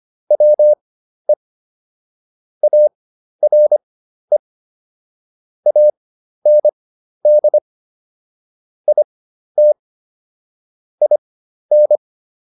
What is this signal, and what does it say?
Morse code, heard as a keyed tone, about 25 wpm, spaced as for 12 wpm, 600 Hz. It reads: WE ARE AND IT IN